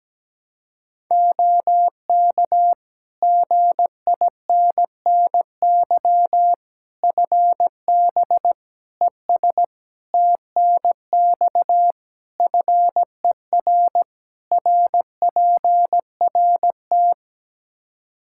Morse code, heard as a keyed tone, about 17 wpm, 700 Hz. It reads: OK GINNY FB ES TNX FER RPRT
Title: Practice QSO - Exchange 2 - Info - 17wpm